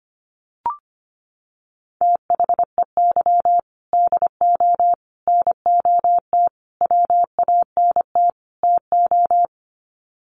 Morse code, heard as a keyed tone, about 25 wpm, 700 Hz. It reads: THEY DO NOT WANT TO
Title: Sets of 5 Words 25wpm